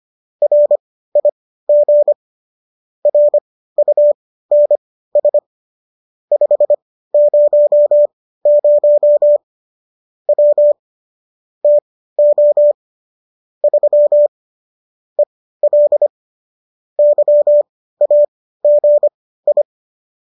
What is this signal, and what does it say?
Morse code, heard as a keyed tone, about 25 wpm, spaced as for 15 wpm, 590 Hz. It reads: RIG RUNS 500 W TO 3 EL YAGI